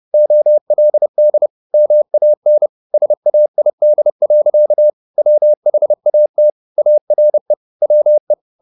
Morse code, heard as a keyed tone, about 30 wpm, 600 Hz. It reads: OLD MAN SAID. WHAT ARE WE
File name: the old man and the sea / the old man and the sea 1 30wpm